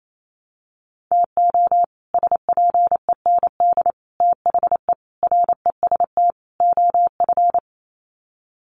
Morse code, heard as a keyed tone, about 28 wpm, 700 Hz. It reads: TO SPEND THE REST OF
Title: Sets of 5 Words Encore2 28wpm